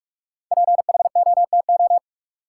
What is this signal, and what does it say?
Morse code, heard as a keyed tone, about 43 wpm, 700 Hz. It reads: PHOTO
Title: Sentences from Top 1200 Words - New Word Drill - 45wpm